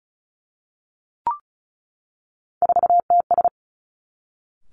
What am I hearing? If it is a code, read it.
Morse code, heard as a keyed tone, about 36 wpm, 700 Hz. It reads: 4TS
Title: Three Challenging Letter-Number Combinations 35wpm